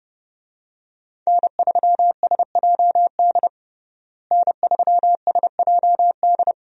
Morse code, heard as a keyed tone, about 30 wpm, 700 Hz. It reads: N3SJD N3SJD
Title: USA Call Signs - 5 Characters - Repeated 3x - 30wpm